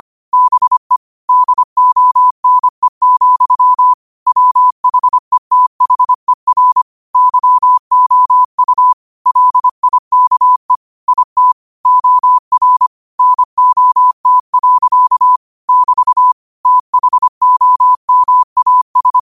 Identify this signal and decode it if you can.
Morse code, heard as a keyed tone, about 25 wpm, 1000 Hz. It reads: BE DONE, WHETHER YOU LIKE IT OR NOT. <BT> THOMAS
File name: qotdfiles / 25wpm